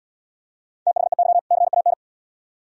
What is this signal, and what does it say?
Morse code, heard as a keyed tone, about 65 wpm, 700 Hz. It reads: THEY CAN